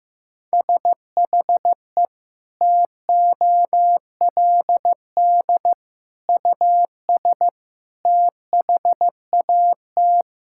Morse code, heard as a keyed tone, about 16 wpm, 700 Hz. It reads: SHE TOLD US THAT
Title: Sentences from Top 1000 Words - Review Part 6 - 15wpm